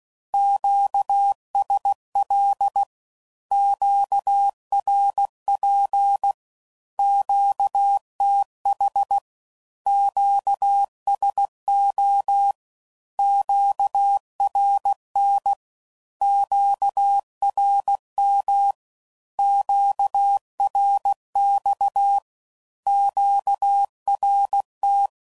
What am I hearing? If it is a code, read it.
Morse code, heard as a keyed tone, about 16 wpm, 800 Hz. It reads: QSL QRP QTH QSO QRN QRM QRX QRT
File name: CW Code Groups @15WPM / Q Calls / Q Calls Once